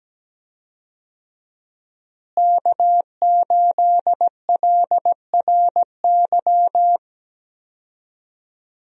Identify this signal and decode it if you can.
Morse code, heard as a keyed tone, about 17 wpm, 700 Hz. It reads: K8LRY
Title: USA Call Signs - 5 Characters - Repeated 3x - 17wpm